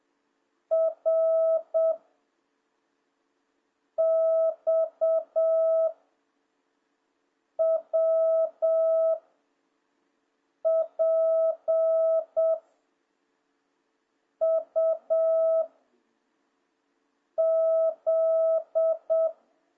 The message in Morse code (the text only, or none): R X W P U Z